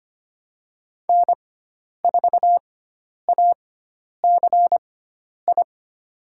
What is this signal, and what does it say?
Morse code, heard as a keyed tone, about 25 wpm, spaced as for 10 wpm, 700 Hz. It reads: N4ACI
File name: Lesson 16g - Letter c - Call Signs 25wpm